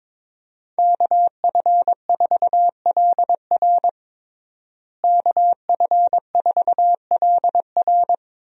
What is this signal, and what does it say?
Morse code, heard as a keyed tone, about 22 wpm, 700 Hz. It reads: KF4LR KF4LR
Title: USA Call Signs - 5 Characters - Repeated 3x - 22wpm